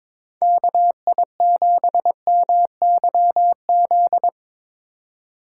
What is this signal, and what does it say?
Morse code, heard as a keyed tone, about 22 wpm, 700 Hz. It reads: KI7MYZ
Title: USA Call Signs - 6 Characters - 3x Character Spacing - 22wpm